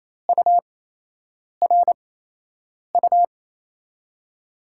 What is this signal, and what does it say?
Morse code, heard as a keyed tone, about 28 wpm, spaced as for 8 wpm, 700 Hz. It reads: URU
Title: USA Call Signs - 8x Character Spacing - 28wpm